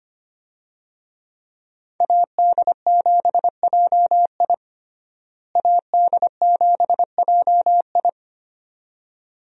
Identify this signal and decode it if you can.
Morse code, heard as a keyed tone, about 25 wpm, 690 Hz. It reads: AD7JI AD7JI
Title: USA Call Signs - 5 Characters - Repeated 2x - 25wpm